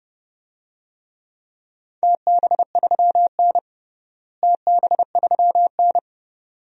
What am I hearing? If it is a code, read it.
Morse code, heard as a keyed tone, about 31 wpm, 700 Hz. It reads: TB3N TB3N
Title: DXCC-100 Call Signs - 4 Characters - Repeated 2x - 30wpm